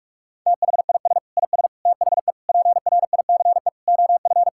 Morse code, heard as a keyed tone, about 42 wpm, 700 Hz. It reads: THIS IS THE PRICE OF
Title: Sentences from Top 1000 Words - Review Part 5 - 45wpm